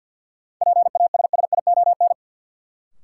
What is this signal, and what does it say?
Morse code, heard as a keyed tone, about 51 wpm, 700 Hz. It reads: PASSION
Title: Sentences from Top 1300 Words - New Word Drill Alt - 50wpm